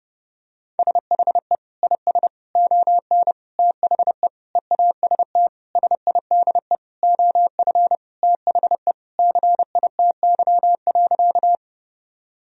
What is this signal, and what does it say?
Morse code, heard as a keyed tone, about 32 wpm, 700 Hz. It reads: SHE IS ON THE EAST SIDE OF THE CITY.
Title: Sentences from Top 1000 Words - Review Part 3 - 30wpm